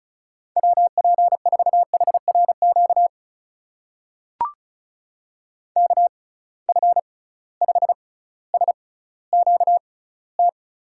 Morse code, heard as a keyed tone, about 34 wpm, spaced as for 18 wpm, 700 Hz. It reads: WP4HRQ KF5SQT
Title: USA Call Signs - 6 Characters - 6x Character Spacing - 35wpm